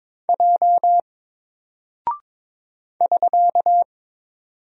Morse code, heard as a keyed tone, about 22 wpm, 700 Hz. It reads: J <SK>
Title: Single Character (Letter:Number:Punctuation:Prosign) 22wpm